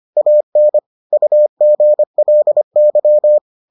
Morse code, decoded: AN UGLY

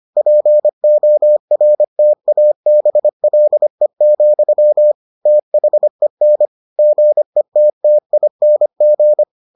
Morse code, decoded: PORTABLE, THEN GETTING

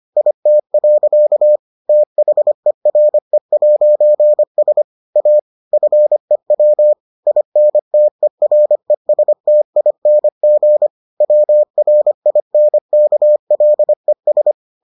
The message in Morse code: IT. THERE'S A FEW INTERESTING WRINKLES

